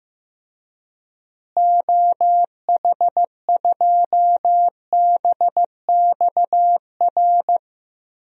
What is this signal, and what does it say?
Morse code, heard as a keyed tone, about 15 wpm, 700 Hz. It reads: OH2BXR
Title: DXCC-100 Call Signs - 6 Characters - Repeated 2x - 15wpm